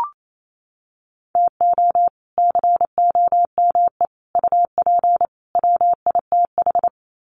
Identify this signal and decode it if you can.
Morse code, heard as a keyed tone, about 28 wpm, 700 Hz. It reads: TO COME UP WITH
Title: Sets of 4 Words 28wpm